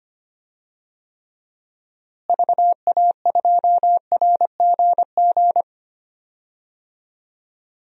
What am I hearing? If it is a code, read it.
Morse code, heard as a keyed tone, about 25 wpm, 700 Hz. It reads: VA2RGG
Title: DXCC-100 Call Signs - 6 Characters - Repeated 2x - 25wpm